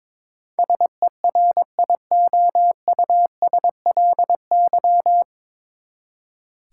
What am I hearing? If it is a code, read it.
Morse code, heard as a keyed tone, about 22 wpm, 700 Hz. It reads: SERIOUSLY
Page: Morse Code Ninja practice 22wpm